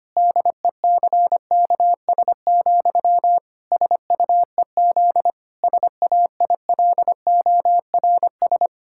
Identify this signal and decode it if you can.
Morse code, heard as a keyed tone, about 25 wpm, 700 Hz. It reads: DECKS, SUEZ SAILORS
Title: Pangram Challenge v2 - 25wpm